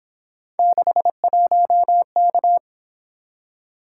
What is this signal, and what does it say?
Morse code, heard as a keyed tone, about 26 wpm, 700 Hz. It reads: 61K
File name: Speed-Racing - Three Letter-Number Combinations - 22wpm